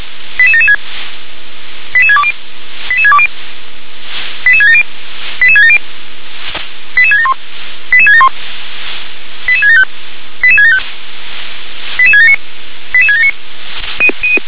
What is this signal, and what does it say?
Five-tone selective call (ZVEI), heard as quick runs of five tones, tones of about 70 ms. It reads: 88776 88620 88620 88670 88670 88671 88671 88675 88675 88679 88679
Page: SDR/ZVEI